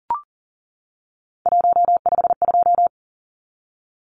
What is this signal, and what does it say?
Morse code, heard as a keyed tone, about 40 wpm, 700 Hz. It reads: 152
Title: Numbers - 3-Digit 40wpm